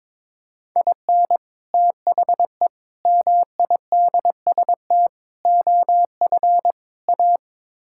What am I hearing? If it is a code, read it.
Morse code, heard as a keyed tone, about 22 wpm, 700 Hz. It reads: IN THE MIDST OF A
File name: practice / 22wpm / Build A Phrase - 5-Word Phrases - 22wpm